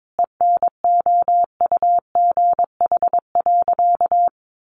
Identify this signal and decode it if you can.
Morse code, heard as a keyed tone, about 22 wpm, 700 Hz. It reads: ENOUGH.
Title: Sentences from Top 1500 Words 22wpm